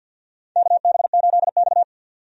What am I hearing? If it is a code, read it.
Morse code, heard as a keyed tone, about 48 wpm, 700 Hz. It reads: KB8X
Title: USA Call Signs - 4 Characters - 50wpm